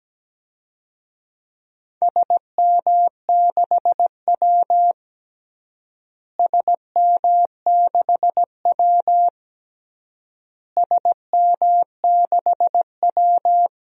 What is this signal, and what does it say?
Morse code, heard as a keyed tone, about 17 wpm, 700 Hz. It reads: SM6W SM6W SM6W
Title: DXCC-100 Call Signs - 4 Characters - Repeated 3x - 17wpm